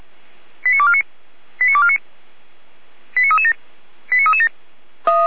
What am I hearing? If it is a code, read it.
Five-tone selective call (ZVEI), heard as quick runs of five tones, tones of about 70 ms. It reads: 89249 89249 89337 89337
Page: ZVEI